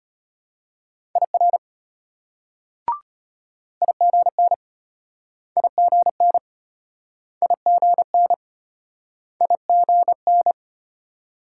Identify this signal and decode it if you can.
Morse code, heard as a keyed tone, about 33 wpm, 700 Hz. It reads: IR IGN IGN IGN IGN